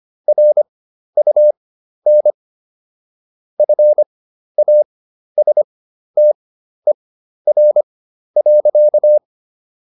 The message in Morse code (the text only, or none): RUN FASTER.